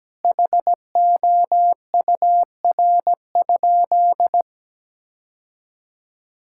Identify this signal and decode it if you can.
Morse code, heard as a keyed tone, about 17 wpm, 700 Hz. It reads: HOUR?
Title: Sentences from Top 1000 Words - Review Part 5 - 17wpm